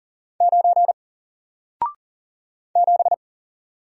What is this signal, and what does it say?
Morse code, heard as a keyed tone, about 39 wpm, 700 Hz. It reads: 9 7
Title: Single Letter-Number Alt 40wpm